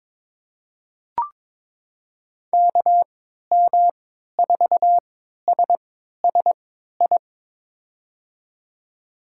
Morse code, heard as a keyed tone, about 21 wpm, spaced as for 12 wpm, 700 Hz. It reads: KM4SSI
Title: USA Call Signs - 6 Characters - 3x Character Spacing - 22wpm